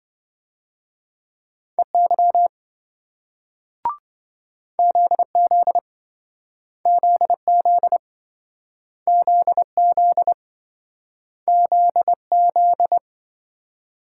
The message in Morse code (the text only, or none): EY ZZ ZZ ZZ ZZ